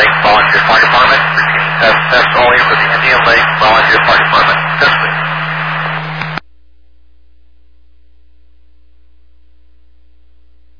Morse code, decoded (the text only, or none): none